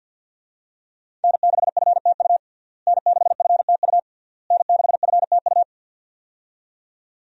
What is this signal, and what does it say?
Morse code, heard as a keyed tone, about 48 wpm, 700 Hz. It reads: N6FTU N6FTU N6FTU
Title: USA Call Signs - 5 Characters - Repeated 3x - 50wpm